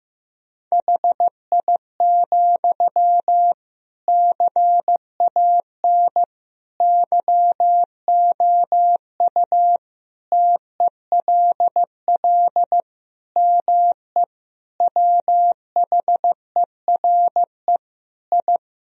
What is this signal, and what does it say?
Morse code, heard as a keyed tone, about 15 wpm, 700 Hz. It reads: HI, CAN YOU TELL ME WHERE I
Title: Sentences from Top 1000 Words - Review Part 3 - 15wpm